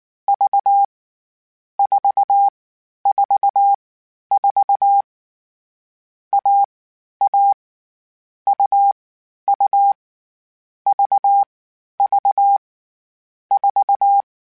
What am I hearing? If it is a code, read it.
Morse code, heard as a keyed tone, about 19 wpm, 800 Hz. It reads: V 4 4 4 A A U U V V 4